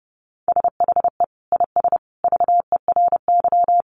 Morse code, heard as a keyed tone, about 29 wpm, 700 Hz. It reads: SHE IS VERY
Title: Easy Phrases 30wpm